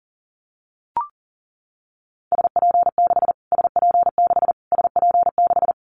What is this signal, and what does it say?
Morse code, heard as a keyed tone, about 40 wpm, 700 Hz. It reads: SP6 SP6 SP6
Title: Callsign Prefixes for Top 25 DXCC - Repeated 3x - 40wpm